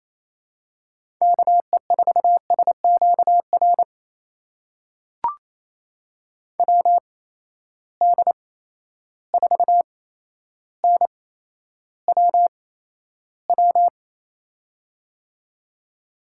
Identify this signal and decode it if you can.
Morse code, heard as a keyed tone, about 28 wpm, spaced as for 12 wpm, 700 Hz. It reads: KE4SQR WD4NWW
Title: USA Call Signs - 6 Characters - 8x Character Spacing - 28wpm